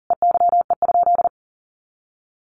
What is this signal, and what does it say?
Morse code, heard as a keyed tone, about 40 wpm, 700 Hz. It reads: EYE?
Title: Sentences from Top 500 Words 40wpm